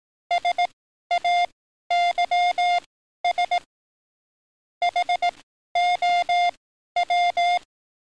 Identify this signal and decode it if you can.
Morse code, heard as a keyed tone, about 18 wpm, spaced as for 12 wpm, 710 Hz. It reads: SAYS HOW
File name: training / NAVPERS 11049to11060 / NAVPERS 11053 RA RB